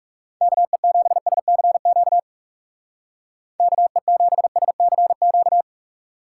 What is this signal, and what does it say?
Morse code, heard as a keyed tone, about 38 wpm, 700 Hz. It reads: KE7SCQ KE7SCQ
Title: Speed Racing - USA 2x3 Call Signs 30wpm